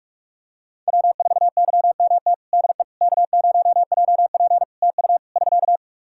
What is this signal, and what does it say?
Morse code, heard as a keyed tone, about 46 wpm, 700 Hz. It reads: W4YMT DE K0JP TU <SK>
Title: Practice QSO - Exchange 4 - Ending - 45wpm